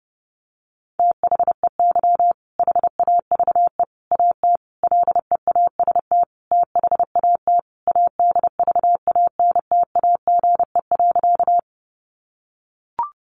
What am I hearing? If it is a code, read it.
Morse code, heard as a keyed tone, about 28 wpm, 700 Hz. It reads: THEY HAVE AT LEAST THAT ADVANTAGE.
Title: Sentences from Top 900 Words 30wpm